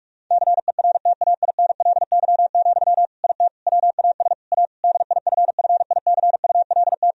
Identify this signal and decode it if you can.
Morse code, heard as a keyed tone, about 46 wpm, 700 Hz. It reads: CERTAINLY, IT WAS A DIFFICULT
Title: Sentences from Top 1000 Words - Review Part 2 - 45wpm